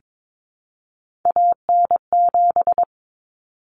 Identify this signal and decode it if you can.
Morse code, heard as a keyed tone, about 22 wpm, 700 Hz. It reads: AN7
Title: Common Cut Numbers - 3-Digit 22wpm